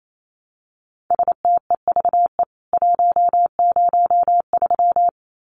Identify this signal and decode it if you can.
Morse code, heard as a keyed tone, about 28 wpm, 700 Hz. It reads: STEVE 103
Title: CWT Exchange 28wpm